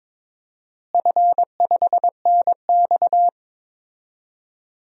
Morse code, heard as a keyed tone, about 22 wpm, 700 Hz. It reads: F5NX
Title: DXCC-100 Call Signs - 4 Characters - Repeated 3x - 22wpm